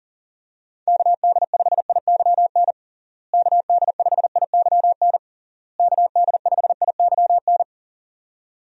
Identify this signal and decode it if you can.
Morse code, heard as a keyed tone, about 40 wpm, 700 Hz. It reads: KD5IYN KD5IYN KD5IYN
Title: USA Call Signs - 6 Characters - Repeated 3x - 40wpm